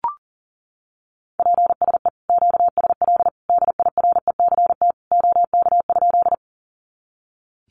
Morse code, heard as a keyed tone, about 40 wpm, 700 Hz. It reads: PSE QSL DIRECT OK?
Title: Common QSO Phrases 40wpm